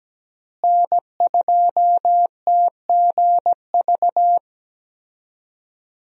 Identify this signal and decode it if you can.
Morse code, heard as a keyed tone, about 17 wpm, 700 Hz. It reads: N2TGV
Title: USA Call Signs - 5 Characters - Repeated 3x - 17wpm